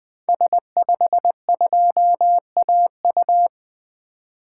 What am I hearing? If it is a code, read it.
Morse code, heard as a keyed tone, about 20 wpm, 700 Hz. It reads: S52AU
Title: 2023 SOTA Activator Call Signs - 20wpm